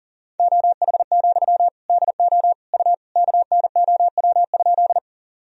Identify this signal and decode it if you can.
Morse code, heard as a keyed tone, about 40 wpm, 700 Hz. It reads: OH, DO U KNOW?
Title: Sentences from Top 500 Words and Abbreviations 40wpm